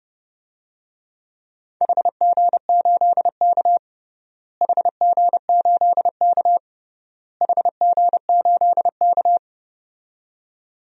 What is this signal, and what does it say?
Morse code, heard as a keyed tone, about 32 wpm, 700 Hz. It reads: HG8K HG8K HG8K
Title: DXCC-100 Call Signs - 4 Characters - Repeated 3x - 30wpm